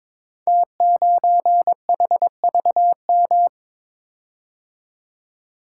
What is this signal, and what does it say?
Morse code, heard as a keyed tone, about 22 wpm, 700 Hz. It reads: T9HVM
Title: Code Groups - 5 Letter:Numbers - 22wpm